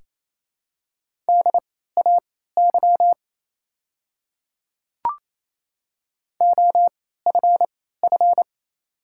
Morse code, heard as a keyed tone, about 29 wpm, spaced as for 16 wpm, 700 Hz. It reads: DAY OFF
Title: 3x Character Spacing - Top 200 Words - 28wpm